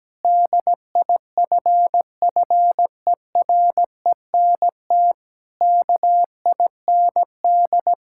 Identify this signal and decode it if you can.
Morse code, heard as a keyed tone, about 17 wpm, 700 Hz. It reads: DIFFERENT KIND
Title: Sentences from Top 1000 Words - Review Part 5 - 17wpm